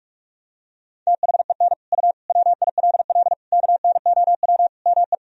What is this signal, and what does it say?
Morse code, heard as a keyed tone, about 46 wpm, 700 Hz. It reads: THEN U WILL KNOW ME
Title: Sentences from Top 400 Words and Abbreviations 45wpm